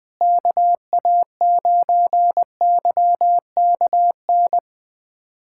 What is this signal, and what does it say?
Morse code, heard as a keyed tone, about 20 wpm, 700 Hz. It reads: KA9YKN
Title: USA Callsign DE Callsign - 6 Characters - 20wpm